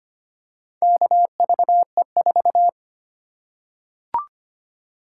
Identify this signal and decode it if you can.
Morse code, heard as a keyed tone, about 25 wpm, 700 Hz. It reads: KVE4
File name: practice / 25wpm / Four Letter-Number Combinations - 25wpm